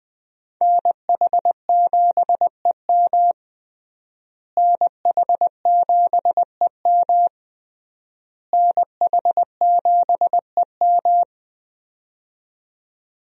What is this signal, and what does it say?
Morse code, heard as a keyed tone, about 20 wpm, 700 Hz. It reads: NH7EM NH7EM NH7EM
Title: USA Call Signs - 5 Characters - Repeated 3x - 20wpm